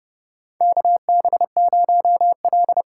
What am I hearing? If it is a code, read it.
Morse code, heard as a keyed tone, about 29 wpm, 700 Hz. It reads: KB0L
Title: USA Call Signs - 4 Characters - 3x Character Spacing - 30wpm